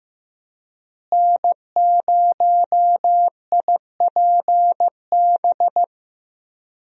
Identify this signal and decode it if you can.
Morse code, heard as a keyed tone, about 15 wpm, 700 Hz. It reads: N0IPB
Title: USA Call Signs - 5 Characters - Repeated 2x - 15wpm